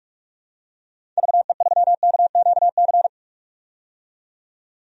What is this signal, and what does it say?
Morse code, heard as a keyed tone, about 46 wpm, 700 Hz. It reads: VE3KQC
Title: DXCC-100 Call Signs - 6 Characters - 45wpm